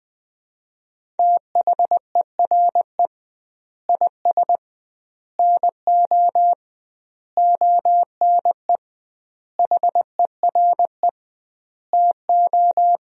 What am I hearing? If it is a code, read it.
Morse code, heard as a keyed tone, about 20 wpm, 700 Hz. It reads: THERE IS NO ONE HERE TO
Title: 2x Word-Spacing - Sentences from Top 200 Words - 20wpm